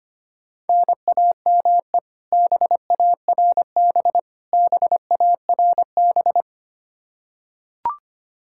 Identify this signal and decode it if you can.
Morse code, heard as a keyed tone, about 25 wpm, 700 Hz. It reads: NAME BARB BARB
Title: Practice QSO - Exchange 1 - Beginning - 25wpm